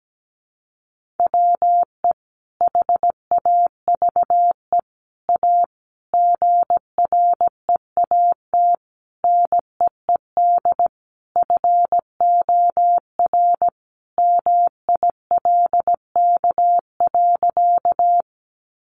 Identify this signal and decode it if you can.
Morse code, heard as a keyed tone, about 17 wpm, 700 Hz. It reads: WE HAVE A GREAT NEED FOR MILK.
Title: Sentences from Top 1500 Words 17wpm